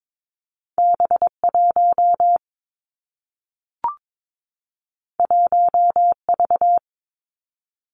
Morse code, heard as a keyed tone, about 22 wpm, 700 Hz. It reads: B 1 1 V
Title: Two Challenging Letter-Number Combinations 22wpm